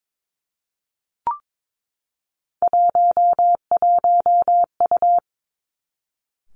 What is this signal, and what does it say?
Morse code, heard as a keyed tone, about 22 wpm, 700 Hz. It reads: 11U
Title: Three Challenging Letter-Number Combinations 22wpm